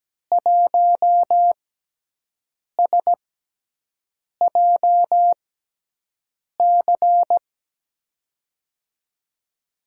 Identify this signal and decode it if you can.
Morse code, heard as a keyed tone, about 17 wpm, spaced as for 6 wpm, 700 Hz. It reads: 1SJC
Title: USA Call Signs - 5 Characters - 6x Character Spacing - 17wpm